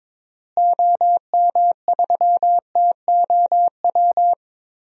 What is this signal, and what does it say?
Morse code, heard as a keyed tone, about 22 wpm, 700 Hz. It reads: OM3TOW
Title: DXCC-100 Call Signs - 6 Characters - Repeated 2x - 22wpm